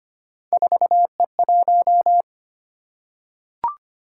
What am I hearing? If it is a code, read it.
Morse code, heard as a keyed tone, about 25 wpm, 700 Hz. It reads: 4E1
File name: Three Letter-Number Combinations - 25wpm